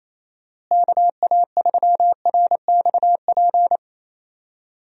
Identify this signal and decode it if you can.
Morse code, heard as a keyed tone, about 28 wpm, 700 Hz. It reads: KA3RXP
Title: USA Call Signs - 6 Characters - Repeated 3x - 28wpm